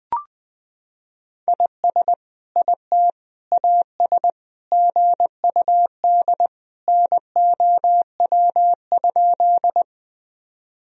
Morse code, heard as a keyed tone, about 20 wpm, 700 Hz. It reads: IS IT AS GUD NOW?
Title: Sentences from Top 200 Words and Abbreviations 20wpm